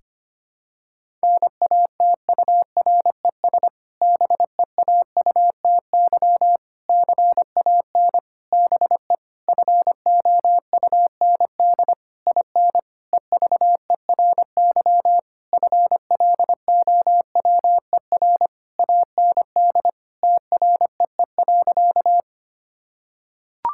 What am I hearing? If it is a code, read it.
Morse code, heard as a keyed tone, about 25 wpm, 700 Hz. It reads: NATURES BEAUTY CAN BE FOUND IN EVERY FLOWER AND TREE.
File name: practice / 25wpm / Difficult Sentences with Hints - 25wpm